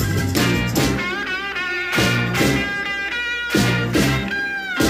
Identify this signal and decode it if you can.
Emergency Alert System (SAME) tones, heard as harsh buzzing bursts. It is heard from 1.1 to 4.7 s.